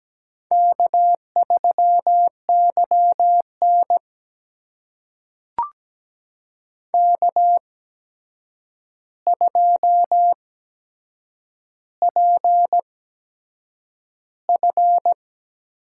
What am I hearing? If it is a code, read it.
Morse code, heard as a keyed tone, about 17 wpm, spaced as for 7 wpm, 700 Hz. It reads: K3YN K2PF
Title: USA Call Signs - 4 Characters - 8x Character Spacing - 17wpm